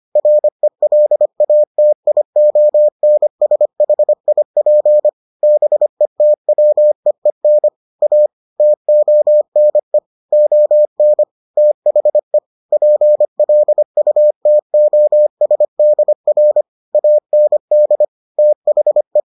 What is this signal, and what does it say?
Morse code, heard as a keyed tone, about 25 wpm, 600 Hz. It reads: RELATIONSHIP BETWEEN A TONE ON THE PLUTOSDR AND THE